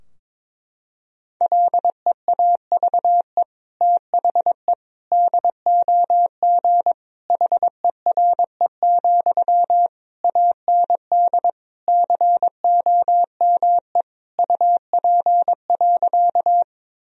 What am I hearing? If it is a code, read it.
Morse code, heard as a keyed tone, about 22 wpm, 700 Hz. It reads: LEAVE THE DOG HERE, AND COME UP.
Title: Sentences from Top 1000 Words - Review Part 2 - 22wpm